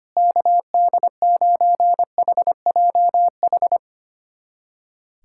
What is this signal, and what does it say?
Morse code, heard as a keyed tone, about 25 wpm, 700 Hz. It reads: KD9HJH
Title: USA Callsign DE Callsign - 6 Characters - 25wpm